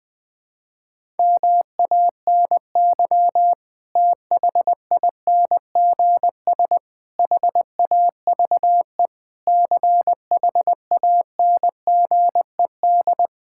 Morse code, heard as a keyed tone, about 20 wpm, 700 Hz. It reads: MANY THINGS HAVE CHANGED
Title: Sentences with 7-Word Phrases - Spoken Beginning - 20wpm